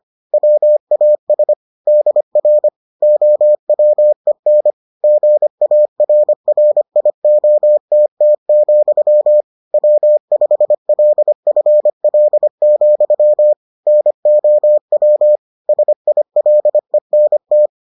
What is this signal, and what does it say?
Morse code, heard as a keyed tone, about 25 wpm, 600 Hz. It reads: WAS DR OWEN GARRIOTT, W5LFL, NOW SILENT